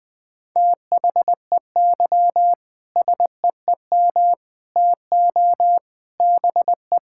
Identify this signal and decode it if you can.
Morse code, heard as a keyed tone, about 20 wpm, 700 Hz. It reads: THEY SEEM TO BE
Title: Sentences with 3-Word Phrases - 20wpm